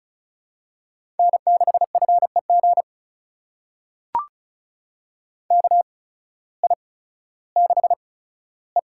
Morse code, heard as a keyed tone, about 36 wpm, spaced as for 16 wpm, 700 Hz. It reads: N6FEG KI6E
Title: USA Call Signs - 5 Characters - 8x Character Spacing - 35wpm